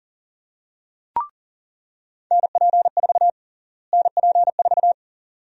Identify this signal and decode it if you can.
Morse code, heard as a keyed tone, about 40 wpm, 700 Hz. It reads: NP4 NP4
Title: USA Call Sign Prefixes - Repeated 2x - 40wpm